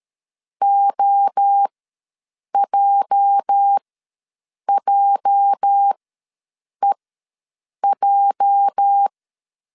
Morse code, heard as a keyed tone, about 13 wpm, spaced as for 7 wpm, 800 Hz. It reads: OJJEJ
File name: LEZIONI / 8 YJ45 / 8 YJ ws 12wpm 3342